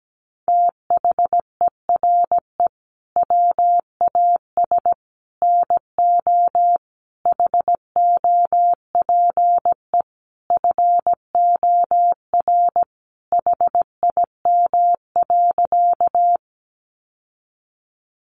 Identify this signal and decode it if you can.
Morse code, heard as a keyed tone, about 17 wpm, 700 Hz. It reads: THERE WAS NO HOPE FOR HIM.
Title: Sentences from Top 1000 Words 17wpm